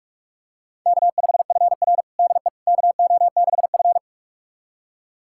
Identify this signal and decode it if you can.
Morse code, heard as a keyed tone, about 41 wpm, 700 Hz. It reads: K5FR DE KO6F
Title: USA Callsign DE Callsign - 4 Characters - 45wpm